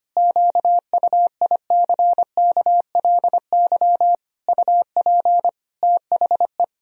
Morse code, heard as a keyed tone, about 25 wpm, 700 Hz. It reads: QUICKLY UP THE
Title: Sentences from Top 1000 Words - Review Part 2 - 25wpm